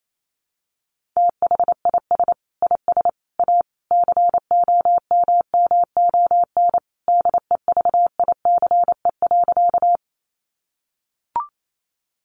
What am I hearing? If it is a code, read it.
Morse code, heard as a keyed tone, about 28 wpm, 700 Hz. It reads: THIS IS A COMMON DEVICE.
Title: Sentences from Top 1500 Words 28wpm